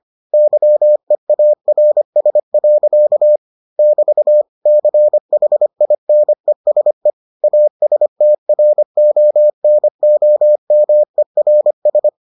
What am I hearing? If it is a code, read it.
Morse code, heard as a keyed tone, about 25 wpm, 600 Hz. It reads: YEARS. <BT> CHINESE ASTRONOMERS